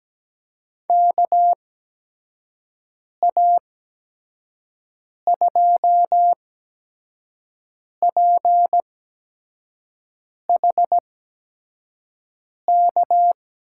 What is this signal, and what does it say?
Morse code, heard as a keyed tone, about 17 wpm, spaced as for 5 wpm, 700 Hz. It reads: KA2PHK
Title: USA Call Signs - 6 Characters - 8x Character Spacing - 17wpm